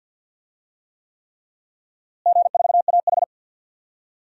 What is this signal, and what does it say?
Morse code, heard as a keyed tone, about 46 wpm, 700 Hz. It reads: G4AH
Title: DXCC-100 Call Signs - 4 Characters - Repeated 3x - 50wpm